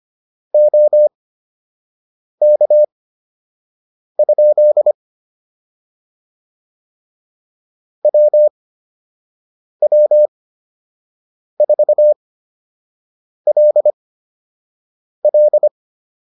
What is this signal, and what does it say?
Morse code, heard as a keyed tone, about 25 wpm, spaced as for 6 wpm, 600 Hz. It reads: OK? WW4LL
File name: QSO 203 06